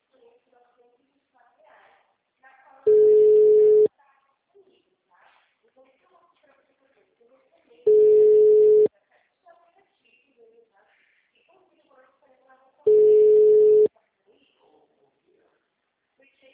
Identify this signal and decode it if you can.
Ringback tone (Europe, 425 Hz), with speech.